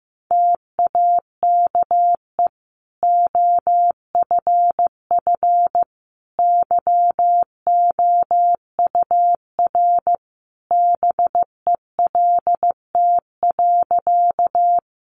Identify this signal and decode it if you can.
Morse code, heard as a keyed tone, about 15 wpm, 700 Hz. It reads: TAKE OFF YOUR BELT.
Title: Sentences from Top 1400 Words 15wpm